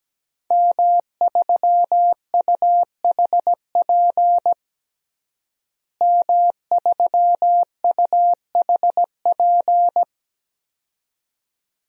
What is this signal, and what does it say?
Morse code, heard as a keyed tone, about 17 wpm, 700 Hz. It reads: M3UHP M3UHP
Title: DXCC-100 Call Signs - 5 Characters - Repeated 2x - 17wpm